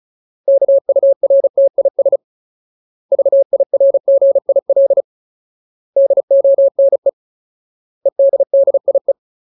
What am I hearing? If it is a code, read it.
Morse code, heard as a keyed tone, about 34 wpm, 550 Hz. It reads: KURTIS VIRGIL DONE EDDIE
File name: Ultimate Word List / 35WPM Ultimate